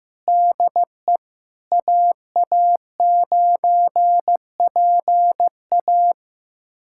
Morse code, heard as a keyed tone, about 15 wpm, 700 Hz. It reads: DE AA9PA
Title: USA Callsign DE Callsign - 5 Characters - 15wpm